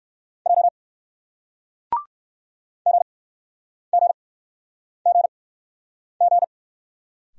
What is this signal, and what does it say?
Morse code, heard as a keyed tone, about 58 wpm, spaced as for 9 wpm, 700 Hz. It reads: J GGGG